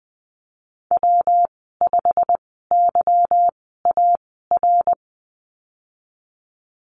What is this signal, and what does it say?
Morse code, heard as a keyed tone, about 20 wpm, spaced as for 15 wpm, 700 Hz. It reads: W5YAR
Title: 186 Lesson 30g - Letter b - Call Signs 20wpm